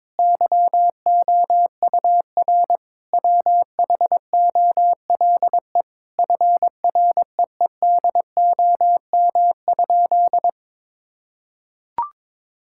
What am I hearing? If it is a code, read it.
Morse code, heard as a keyed tone, about 22 wpm, 700 Hz. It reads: YOUR WHOLE FREEDOM?
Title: Sentences from Top 1000 Words - Review Part 5 - 22wpm